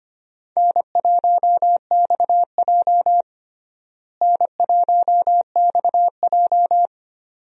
Morse code, heard as a keyed tone, about 25 wpm, 700 Hz. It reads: N1XJ N1XJ
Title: USA Call Signs - 4 Characters - Repeated 2x - 25wpm